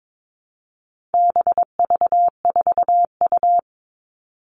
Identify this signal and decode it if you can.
Morse code, heard as a keyed tone, about 22 wpm, 700 Hz. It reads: BV4U
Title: All Cut Numbers - 4-Digit 22wpm